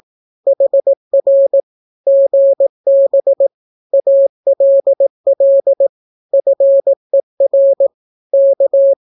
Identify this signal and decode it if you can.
Morse code, heard as a keyed tone, about 18 wpm, 550 Hz. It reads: HR GB ALL FER K
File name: naukapodstaw / skroty / skroty